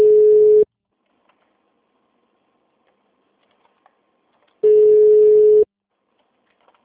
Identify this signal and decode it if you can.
Ringback tone (Europe, 425 Hz).
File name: out- -2007- - - .5